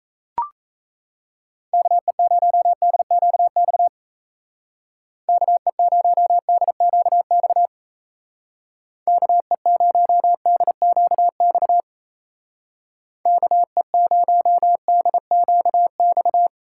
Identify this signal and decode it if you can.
Morse code, heard as a keyed tone, about 34 wpm, 700 Hz. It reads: KE0DQX KE0DQX KE0DQX KE0DQX
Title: Speed Racing - USA 2x3 Call Signs 28wpm